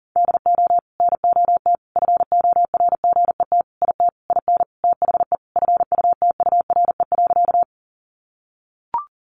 Morse code, heard as a keyed tone, about 40 wpm, 700 Hz. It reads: DO NOT FORGET IT IN THE FUTURE.
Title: Sentences from Top 1200 Words 40wpm